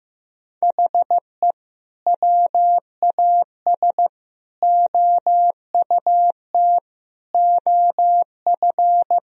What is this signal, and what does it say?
Morse code, heard as a keyed tone, about 15 wpm, 700 Hz. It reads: HE WAS OUT OF
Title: Sentences from Top 1000 Words - Review Part 4 - 15wpm